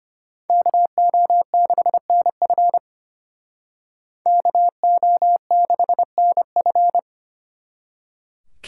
Morse code, heard as a keyed tone, about 31 wpm, 700 Hz. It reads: KO6NF KO6NF
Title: Speed Racing - USA 2x2 Call Signs 25wpm